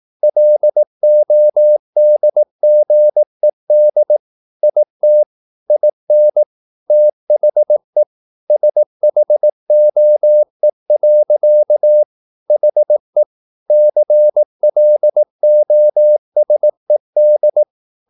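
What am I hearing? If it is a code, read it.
Morse code, heard as a keyed tone, about 18 wpm, 600 Hz. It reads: LODGED IT IN THE SHOE. HE CLOSED
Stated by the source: Morse practice at 18 wpm, English text, from CW med Gnister.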